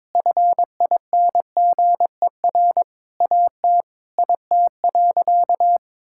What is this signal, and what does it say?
Morse code, heard as a keyed tone, about 22 wpm, 700 Hz. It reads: FINGER AT IT.
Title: Sentences from Top 1600 Words 22wpm